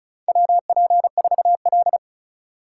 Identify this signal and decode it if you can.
Morse code, heard as a keyed tone, about 36 wpm, 700 Hz. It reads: WP4L
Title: USA Call Signs - 4x Character Spacing - 35wpm